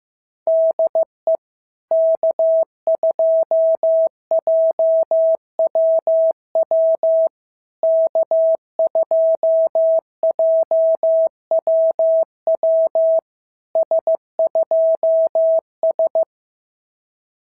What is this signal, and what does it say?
Morse code, heard as a keyed tone, about 15 wpm, 650 Hz. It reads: DE K2JWW K2JWW S2S
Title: SOTA QSO Practice 15wpm